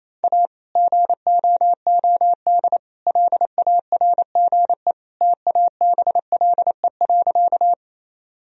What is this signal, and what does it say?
Morse code, heard as a keyed tone, about 28 wpm, 700 Hz. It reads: A GOOD LARGE TABLE.
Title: Sentences from Top 1000 Words - Review Part 2 - 28wpm